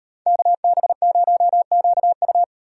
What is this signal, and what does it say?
Morse code, heard as a keyed tone, about 33 wpm, 700 Hz. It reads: KB0QU